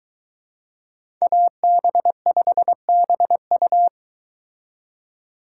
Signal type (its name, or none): Morse code